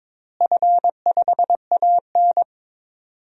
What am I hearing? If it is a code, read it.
Morse code, heard as a keyed tone, about 22 wpm, 700 Hz. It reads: F5AN
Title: DXCC-100 Call Signs - 4 Characters - Repeated 3x - 22wpm